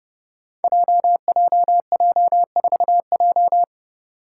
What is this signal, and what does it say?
Morse code, heard as a keyed tone, about 30 wpm, 700 Hz. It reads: JJJ4J
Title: Five Challenging Letter-Number Combinations 30wpm